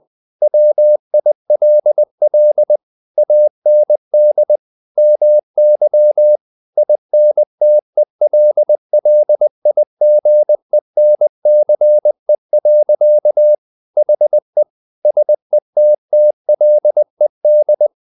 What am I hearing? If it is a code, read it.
Morse code, heard as a keyed tone, about 20 wpm, 600 Hz. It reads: WILL AND MY INTELLIGENCE. HE SETTLED
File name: the old man and the sea / the old man and the sea part 1 20wpm